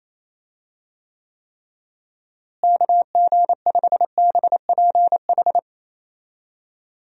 Morse code, heard as a keyed tone, about 28 wpm, 700 Hz. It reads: KG5BPH
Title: USA Call Signs - 6 Characters - Repeated 3x - 28wpm